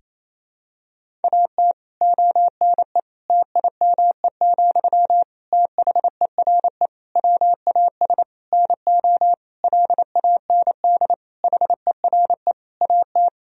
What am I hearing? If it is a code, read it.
Morse code, heard as a keyed tone, about 28 wpm, 700 Hz. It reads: AT ONE TIME, THERE WAS NO LAND HERE AT
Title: Sentences from Top 1000 Words - Review Part 1 - 28wpm